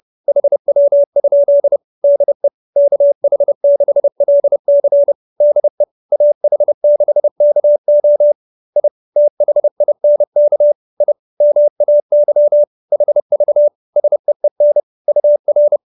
Morse code, heard as a keyed tone, about 30 wpm, 590 Hz. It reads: HW? DE KH6LC DE AH6KO I THINK I MAY HV SEEN UR
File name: QSO311 30